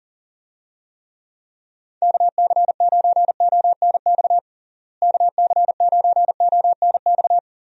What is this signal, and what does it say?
Morse code, heard as a keyed tone, about 40 wpm, 700 Hz. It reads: KC9ONX KC9ONX
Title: USA Call Signs - 6 Characters - Repeated 3x - 40wpm